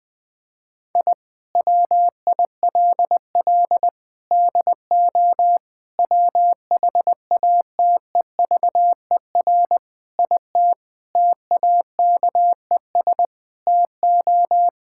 Morse code, heard as a keyed tone, about 20 wpm, 700 Hz. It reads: I WILL DO WHATEVER IT TAKES TO
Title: Sentences with 6-Word Phrases - 20wpm